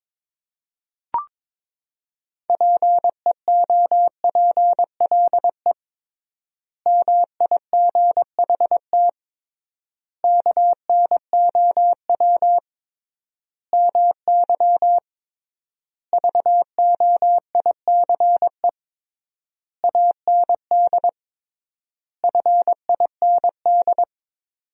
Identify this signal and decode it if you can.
Morse code, heard as a keyed tone, about 22 wpm, 700 Hz. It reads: PEOPLE MIGHT KNOW MY VOICE AND FIND
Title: 3x Word-Spacing - Sentences from Top 500 Words - 22wpm